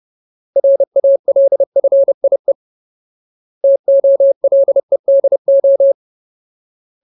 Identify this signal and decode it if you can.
Morse code, heard as a keyed tone, about 30 wpm, 550 Hz. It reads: RALFIE TOLEDO
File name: Ultimate Word List / 30WPM Ultimate